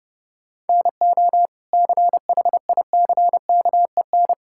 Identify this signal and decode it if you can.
Morse code, heard as a keyed tone, about 30 wpm, 700 Hz. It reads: NO CHICKEN
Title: Sentences from Top 1000 Words - Review Part 5 - 30wpm